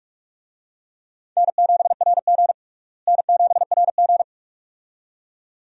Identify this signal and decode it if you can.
Morse code, heard as a keyed tone, about 48 wpm, 700 Hz. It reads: N7RG N7RG
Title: USA Call Signs - 4 Characters - Repeated 2x - 45wpm